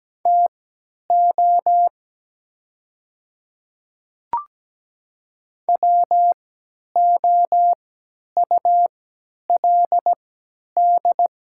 Morse code, heard as a keyed tone, about 17 wpm, spaced as for 10 wpm, 700 Hz. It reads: TO WOULD
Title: 3x Character Spacing - Top 100 Words - 17wpm